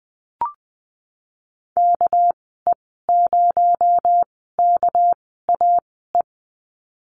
Morse code, heard as a keyed tone, about 20 wpm, spaced as for 14 wpm, 700 Hz. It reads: KE0KAE
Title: 231 Lesson 37g - Letter x - Call Signs 20wpm